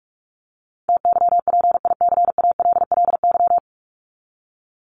Morse code, heard as a keyed tone, about 44 wpm, 700 Hz. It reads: TYPICALLY
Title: LY Suffix - Top 100 Words with Base Words 45wpm